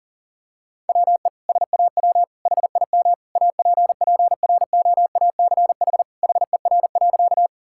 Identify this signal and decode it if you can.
Morse code, heard as a keyed tone, about 41 wpm, 700 Hz. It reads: WE SAW HIM APPROACH HER.
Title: Sentences from Top 1000 Words - Review Part 6 - 40wpm